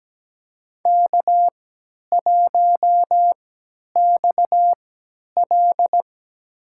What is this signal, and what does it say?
Morse code, heard as a keyed tone, about 17 wpm, spaced as for 10 wpm, 700 Hz. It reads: K1XL
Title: USA Call Signs - 4 Characters - 3x Character Spacing - 17wpm